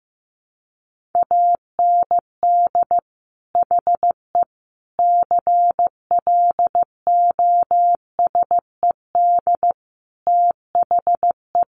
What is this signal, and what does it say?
Morse code, heard as a keyed tone, about 15 wpm, 700 Hz. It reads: AND HE CLOSED THE
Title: Sentences from Top 1500 Words 15wpm